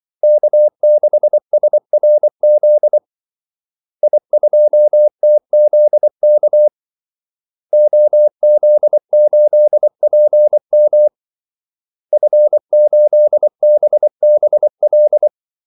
Morse code, heard as a keyed tone, about 24 wpm, 600 Hz. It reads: K6SRZ I2TZK OZ8PM F8BBL
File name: Callsigns-24wpm